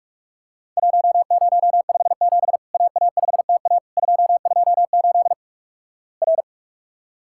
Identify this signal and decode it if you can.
Morse code, heard as a keyed tone, about 48 wpm, 700 Hz. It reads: 1057 AA5TA 228 R
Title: Worked All Europe DX Contest - QTC Exchange - 45wpm